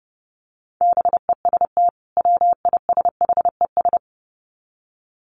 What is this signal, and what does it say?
Morse code, heard as a keyed tone, about 30 wpm, 700 Hz. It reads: BEST WISHES
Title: Common QSO Phrases 30wpm